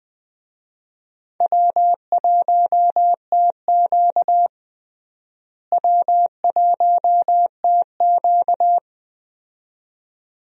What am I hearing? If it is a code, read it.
Morse code, heard as a keyed tone, about 20 wpm, 700 Hz. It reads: W1TQ W1TQ
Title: USA Call Signs - 4 Characters - Repeated 2x - 20wpm